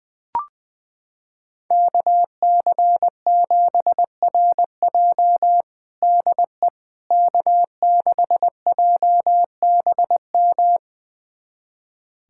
Morse code, heard as a keyed tone, about 20 wpm, 700 Hz. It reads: KC7RJ DE K6JBM
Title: USA Callsign DE Callsign - 5 Characters - 20wpm